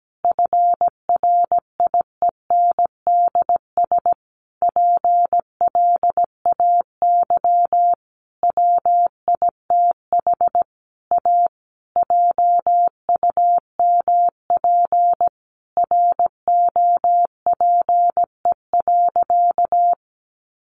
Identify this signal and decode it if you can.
Morse code, heard as a keyed tone, about 17 wpm, 700 Hz. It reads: FRIENDS PLAY WITH A JUMP ROPE.
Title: Easy Phrases 17wpm